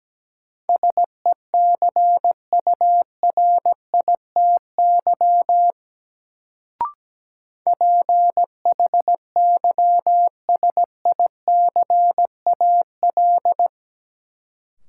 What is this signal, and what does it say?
Morse code, heard as a keyed tone, about 17 wpm, 700 Hz. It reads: SECURITY PHYSICAL
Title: Sentences from Top 1000 Words - Review Part 4 - New Word Drill - 17wpm